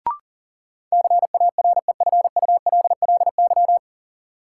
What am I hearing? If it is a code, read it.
Morse code, heard as a keyed tone, about 40 wpm, 700 Hz. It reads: CAREFULLY
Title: Sentences from Top 1300 Words - New Word Drill Alt - 40wpm